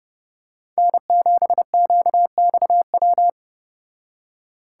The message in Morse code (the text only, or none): N7QXW